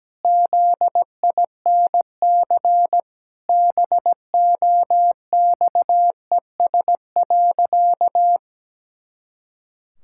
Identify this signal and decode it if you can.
Morse code, heard as a keyed tone, about 17 wpm, 700 Hz. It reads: ZINC BOXES.
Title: Pangram Challenge v2 - 17wpm